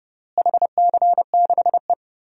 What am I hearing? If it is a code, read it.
Morse code, heard as a keyed tone, about 28 wpm, 700 Hz. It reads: HC6E
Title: DXCC-100 Call Signs - 4 Characters - Repeated 3x - 30wpm